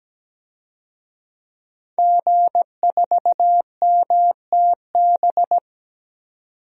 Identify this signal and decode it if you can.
Morse code, heard as a keyed tone, about 17 wpm, 700 Hz. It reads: G4MTB
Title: DXCC-100 Call Signs - 5 Characters - 17wpm